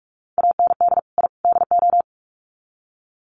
Morse code, heard as a keyed tone, about 44 wpm, 700 Hz. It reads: AND I DO
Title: Sets of 3 Words 45wpm